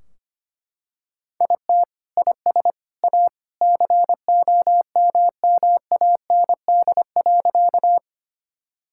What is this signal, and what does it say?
Morse code, heard as a keyed tone, about 25 wpm, 700 Hz. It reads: IT IS A COMMAND.
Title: Sentences from Top 1600 Words 25wpm